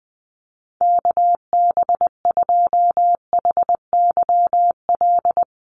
Morse code, heard as a keyed tone, about 20 wpm, 700 Hz. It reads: KB2HYL
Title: USA Call Signs Repeated 2x 20wpm